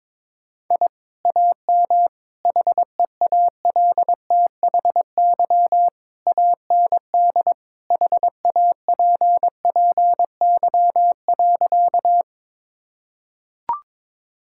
Morse code, heard as a keyed tone, about 22 wpm, 700 Hz. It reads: I AM HEALTHY AND HAPPY.
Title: Sentences from Top 1000 Words - Review Part 5 - 22wpm